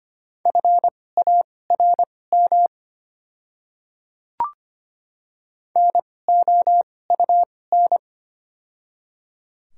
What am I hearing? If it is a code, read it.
Morse code, heard as a keyed tone, about 25 wpm, spaced as for 18 wpm, 700 Hz. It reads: FARM NOUN
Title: 2x Character Spacing - Top 500 Words - 25wpm